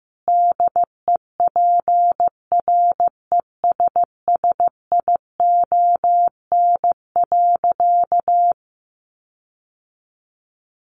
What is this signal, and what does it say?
Morse code, heard as a keyed tone, about 15 wpm, 700 Hz. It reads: DEPRESSION.
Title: Sentences from Top 1300 Words 15wpm